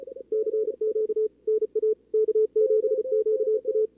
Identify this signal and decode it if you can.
Morse code, heard as a keyed tone, about 39 wpm, 440 Hz. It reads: CQ NA K7QA
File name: CWSprint / Feb2022 / k5tr-feb-na-sprint-cw-k5ot -2300